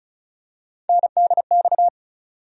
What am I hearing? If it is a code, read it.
Morse code, heard as a keyed tone, about 36 wpm, 700 Hz. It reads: NDX
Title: USA Call Sign Postfixes - 3 Letters 35wpm